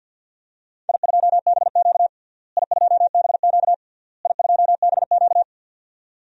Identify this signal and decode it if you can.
Morse code, heard as a keyed tone, about 53 wpm, 700 Hz. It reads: I2BQ I2BQ I2BQ
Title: DXCC-100 Call Signs - 4 Characters - Repeated 3x - 50wpm